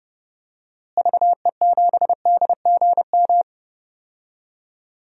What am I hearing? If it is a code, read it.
Morse code, heard as a keyed tone, about 30 wpm, 700 Hz. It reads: VE7DGM
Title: DXCC-100 Call Signs - 6 Characters - Repeated 2x - 30wpm